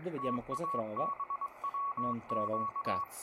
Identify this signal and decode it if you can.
Morse code, heard as a keyed tone, about 21 wpm, 1120 Hz, with speech and radio noise. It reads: TABWQ